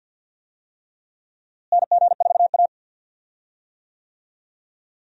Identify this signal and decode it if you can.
Morse code, heard as a keyed tone, about 50 wpm, 700 Hz. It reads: NG4A
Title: USA Call Signs - 4 Characters - Repeated 3x - 50wpm